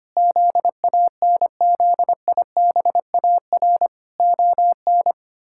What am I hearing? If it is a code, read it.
Morse code, heard as a keyed tone, about 25 wpm, 700 Hz. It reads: ZANZIBAR ON
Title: Pangram Challenge v2 - 25wpm